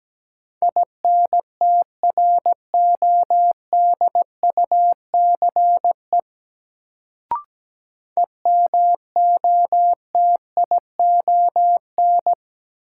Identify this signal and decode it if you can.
Morse code, heard as a keyed tone, about 17 wpm, 700 Hz. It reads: INTRODUCE EMOTION
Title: Sentences from Top 1600 Words - New Word Drill - 17wpm